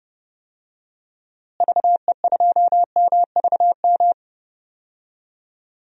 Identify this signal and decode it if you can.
Morse code, heard as a keyed tone, about 30 wpm, 700 Hz. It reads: VE2MVM
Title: DXCC-100 Call Signs - 6 Characters - Repeated 2x - 30wpm